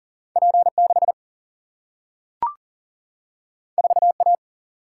Morse code, heard as a keyed tone, about 40 wpm, 700 Hz. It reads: P6 4A
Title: Single-Letter Single-Number 40wpm